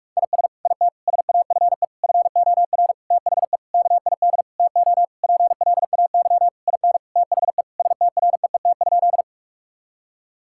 Morse code, heard as a keyed tone, about 43 wpm, 700 Hz. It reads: IS IT SAFE FOR THE KID TO PLAY IN THE STREET?
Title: Sentences from Top 1600 Words 45wpm